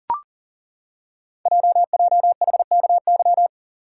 Morse code, heard as a keyed tone, about 40 wpm, 700 Hz. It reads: JJHKY
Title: Code Groups - 5 Letters - 40wpm